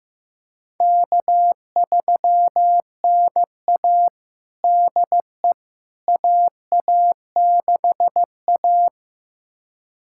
Morse code, heard as a keyed tone, about 15 wpm, 700 Hz. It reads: K3NA DE AA6A
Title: USA Callsign DE Callsign - 4 Characters - 15wpm